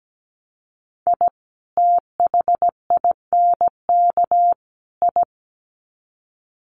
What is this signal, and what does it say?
Morse code, heard as a keyed tone, about 17 wpm, 700 Hz. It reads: I THINK I
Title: Sets of 3 Words 17wpm